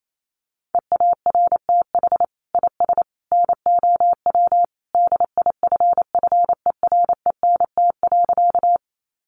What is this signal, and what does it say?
Morse code, heard as a keyed tone, about 28 wpm, 700 Hz. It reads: EARTH IS NOW DIFFERENT.
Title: Sentences from Top 600 Words 28wpm